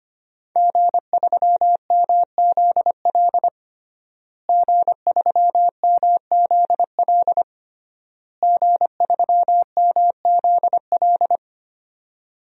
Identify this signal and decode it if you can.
Morse code, heard as a keyed tone, about 25 wpm, 700 Hz. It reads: G3MZL G3MZL G3MZL
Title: DXCC-100 Call Signs - 5 Characters - Repeated 3x - 25wpm